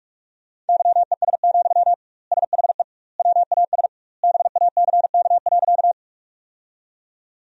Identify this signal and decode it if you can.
Morse code, heard as a keyed tone, about 46 wpm, 700 Hz. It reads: YES, SHE WAS BACK.
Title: 2x Word-Spacing - Sentences from Top 500 Words - 45wpm